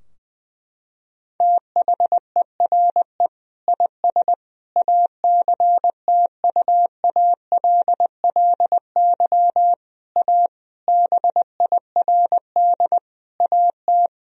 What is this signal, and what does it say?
Morse code, heard as a keyed tone, about 21 wpm, 700 Hz. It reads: THERE IS ACTUALLY A BIRD AT
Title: Sentences from Top 1000 Words - Review Part 4 - 20wpm